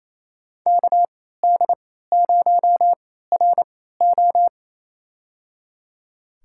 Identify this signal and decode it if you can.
Morse code, heard as a keyed tone, about 28 wpm, spaced as for 16 wpm, 700 Hz. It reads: KD0RO
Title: USA Call Signs - 3x Character Spacing - 28wpm